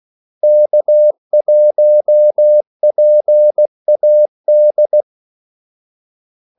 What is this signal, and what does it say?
Morse code, heard as a keyed tone, about 16 wpm, 600 Hz. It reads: K1PAD
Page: Kaldesignaler 16 wpm